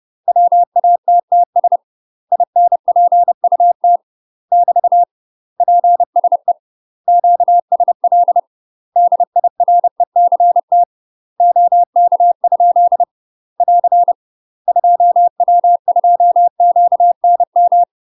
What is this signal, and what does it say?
Morse code, heard as a keyed tone, about 30 wpm, 700 Hz. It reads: WATTS INPUT <BT> PSE QSL DIRECT OK? <AR> 2W2QNM